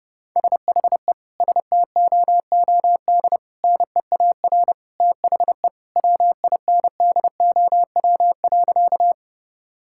Morse code, heard as a keyed tone, about 30 wpm, 700 Hz. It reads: SHE STOOD NEAR THE WINDOW.
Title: Sentences from Top 1600 Words 30wpm